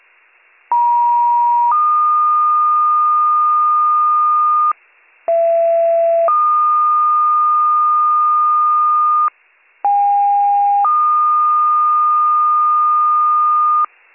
Two-tone page, heard as a one-second tone, then a three-second tone, about 950 then 1220 Hz, about 670 then 1220 Hz, about 810 then 1220 Hz. It is heard from 0.7 to 13.9 s.